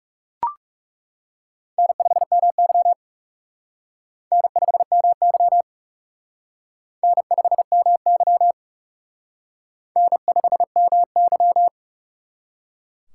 Morse code, heard as a keyed tone, about 39 wpm, 700 Hz. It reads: N5MY N5MY N5MY N5MY